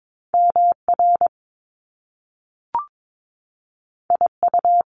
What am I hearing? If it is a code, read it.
Morse code, heard as a keyed tone, about 22 wpm, 700 Hz. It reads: MR IU
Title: Top Two Letter Combinations in Words 22wpm